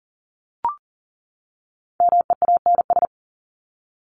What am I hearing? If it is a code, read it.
Morse code, heard as a keyed tone, about 40 wpm, 700 Hz. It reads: MEANS
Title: ES:S Suffix - Top 100 Words with Base Words 40wpm